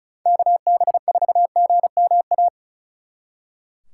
Morse code, heard as a keyed tone, about 43 wpm, 700 Hz. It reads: KB4GMA
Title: Speed Racing - USA 2x3 Call Signs 35wpm